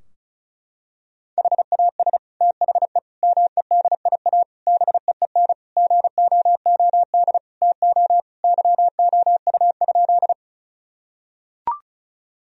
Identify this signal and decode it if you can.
Morse code, heard as a keyed tone, about 34 wpm, 700 Hz. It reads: HAS THE MEDIA BEEN GOOD TO YOU?
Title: Sentences from Top 1000 Words - Review Part 4 - 35wpm